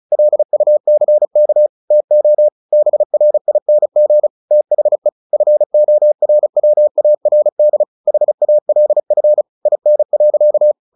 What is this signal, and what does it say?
Morse code, heard as a keyed tone, about 34 wpm, 600 Hz. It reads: LUCK TO BRING THE FORWARD HALF IN.
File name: the old man and the sea 2 35wpm